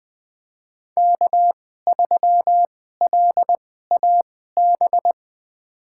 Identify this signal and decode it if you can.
Morse code, heard as a keyed tone, about 20 wpm, spaced as for 15 wpm, 700 Hz. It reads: K3LAB
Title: USA Call Signs - 5 Characters - 2x Character Spacing - 20wpm